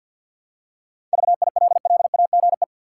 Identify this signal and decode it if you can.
Morse code, heard as a keyed tone, about 50 wpm, 700 Hz. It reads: VILLAGE
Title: Seven Letter Words - 50wpm